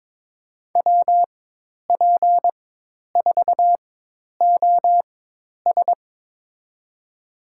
Morse code, heard as a keyed tone, about 22 wpm, spaced as for 10 wpm, 700 Hz. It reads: WP4OS
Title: USA Call Signs - 4x Character Spacing - 22wpm